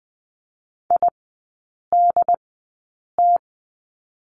Morse code, heard as a keyed tone, about 20 wpm, spaced as for 8 wpm, 700 Hz. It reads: IDT
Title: 067 Lesson 12e - Letter d - Three Characters 20wpm